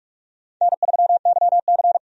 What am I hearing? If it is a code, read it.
Morse code, heard as a keyed tone, about 43 wpm, 700 Hz. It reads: N3YC
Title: USA Call Signs - 4 Characters - 45wpm